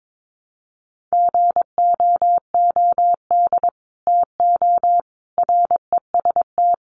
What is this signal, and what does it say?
Morse code, heard as a keyed tone, about 22 wpm, 700 Hz. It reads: GOOD TO REST
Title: Sentences from Top 500 Words 22wpm